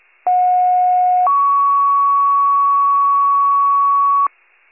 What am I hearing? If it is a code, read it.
Two-tone page, heard as a one-second tone, then a three-second tone, about 730 then 1120 Hz. It is heard from 0.3 to 4.3 s.
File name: wcalert